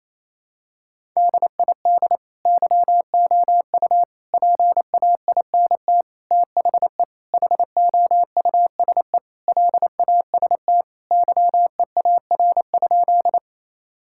Morse code, heard as a keyed tone, about 28 wpm, 700 Hz. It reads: DID YOU PAINT THE HOUSE LAST YEAR?
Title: Sentences from Top 1000 Words - Review Part 6 - 28wpm